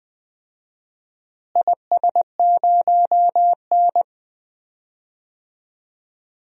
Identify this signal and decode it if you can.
Morse code, heard as a keyed tone, about 20 wpm, 700 Hz. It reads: IS0N
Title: DXCC-100 Call Signs - 4 Characters - Repeated 2x - 20wpm